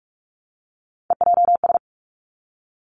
Morse code, heard as a keyed tone, about 46 wpm, 700 Hz. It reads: EJS
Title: Three Challenging Letter-Number Combinations 45wpm